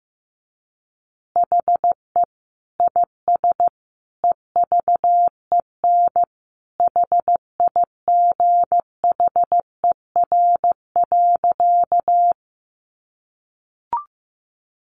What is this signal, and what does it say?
Morse code, heard as a keyed tone, about 15 wpm, 700 Hz. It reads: HE IS EVEN HIGHER.
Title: Additional Easy Phrases 15wpm